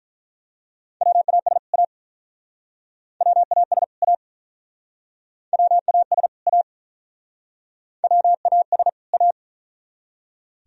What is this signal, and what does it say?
Morse code, heard as a keyed tone, about 47 wpm, 700 Hz. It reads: WAS A WAS A WAS A WAS A